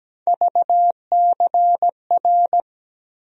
Morse code, heard as a keyed tone, about 17 wpm, 700 Hz. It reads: VCR
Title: USA Call Sign Postfixes - 3 Letters 17wpm